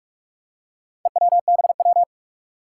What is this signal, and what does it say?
Morse code, heard as a keyed tone, about 46 wpm, 700 Hz. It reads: EWBW
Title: Four Challenging Letter-Number Combinations 45wpm